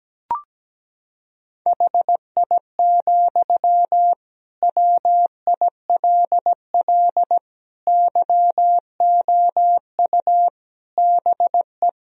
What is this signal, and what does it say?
Morse code, heard as a keyed tone, about 17 wpm, 700 Hz. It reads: HI, WILL YOU BE
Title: Sentences from Top 1000 Words - Review Part 3 - 17wpm